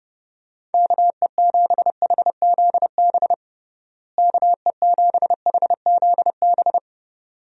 Morse code, heard as a keyed tone, about 32 wpm, 700 Hz. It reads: KE7HZB KE7HZB
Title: USA Call Signs - 6 Characters - Repeated 2x - 30wpm